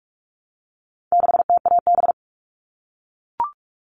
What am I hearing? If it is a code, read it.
Morse code, heard as a keyed tone, about 44 wpm, 700 Hz. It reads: 6TAB
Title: All Cut Numbers - 4-Digit 45wpm